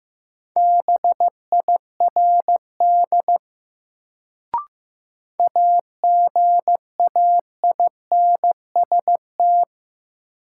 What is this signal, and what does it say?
Morse code, heard as a keyed tone, about 15 wpm, 700 Hz. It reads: BIRD AGAINST
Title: Sentences from Top 400 Words - New Word Drill - 15wpm